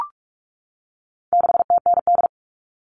Morse code, heard as a keyed tone, about 44 wpm, 700 Hz. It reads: 6TND
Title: All Cut Numbers - 4-Digit 45wpm